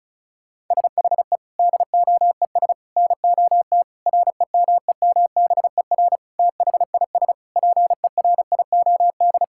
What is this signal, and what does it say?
Morse code, heard as a keyed tone, about 36 wpm, 700 Hz. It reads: SHE DOES NOT REMEMBER THIS PERIOD
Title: Sentences from Top 1000 Words - Review Part 5 - 35wpm